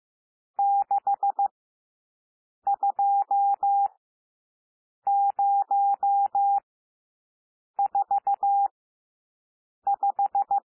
Morse code, heard as a keyed tone, about 15 wpm, spaced as for 6 wpm, 800 Hz. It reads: 62045
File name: SOUND NUMBERS 15 wpm